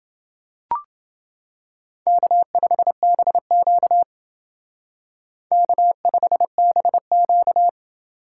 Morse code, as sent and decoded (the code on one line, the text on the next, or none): -.- ..... -... --.- / -.- ..... -... --.-
K5BQ K5BQ